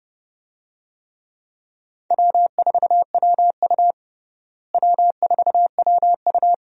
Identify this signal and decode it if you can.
Morse code, heard as a keyed tone, about 30 wpm, 700 Hz. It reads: W4WU W4WU
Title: USA Call Signs - 4 Characters - Repeated 2x - 30wpm